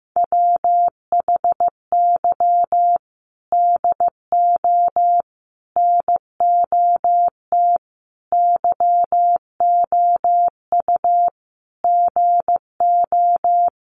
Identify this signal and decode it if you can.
Morse code, heard as a keyed tone, about 15 wpm, 700 Hz. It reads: WHY DO NOT YOU GO
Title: Sets of 5 Words Encore 15wpm